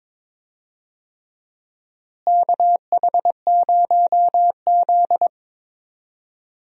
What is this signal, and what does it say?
Morse code, heard as a keyed tone, about 22 wpm, 700 Hz. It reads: KH0Z
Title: USA Call Signs - 4 Characters - Repeated 3x - 22wpm